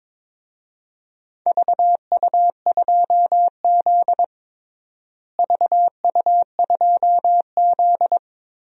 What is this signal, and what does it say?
Morse code, heard as a keyed tone, about 22 wpm, 700 Hz. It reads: VU2Z VU2Z
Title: DXCC-100 Call Signs - 4 Characters - Repeated 3x - 22wpm